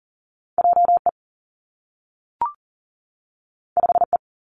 Morse code, heard as a keyed tone, about 40 wpm, 700 Hz. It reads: J E 5 E
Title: Two Challenging Letter-Number Combinations 40wpm